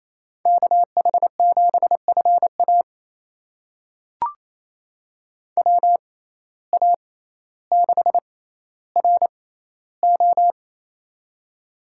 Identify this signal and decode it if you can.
Morse code, heard as a keyed tone, about 28 wpm, spaced as for 14 wpm, 700 Hz. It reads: KH7FA WA6RO
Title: USA Call Signs - 5 Characters - 6x Character Spacing - 28wpm